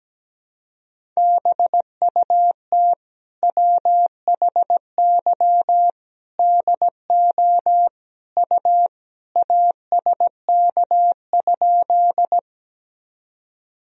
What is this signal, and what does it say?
Morse code, heard as a keyed tone, about 17 wpm, 700 Hz. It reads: BUT WHY DO U ASK?
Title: Sentences from Top 200 Words and Abbreviations 17wpm